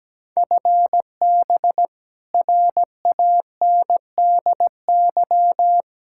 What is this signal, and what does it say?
Morse code, heard as a keyed tone, about 17 wpm, 700 Hz. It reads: FB RANDY
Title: K1USN SST QSO Practice 17wpm